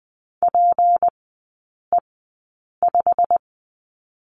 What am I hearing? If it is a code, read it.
Morse code, heard as a keyed tone, about 20 wpm, spaced as for 8 wpm, 690 Hz. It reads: PE5
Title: 159 Lesson 26e - Letter g - Three Characters 20wpm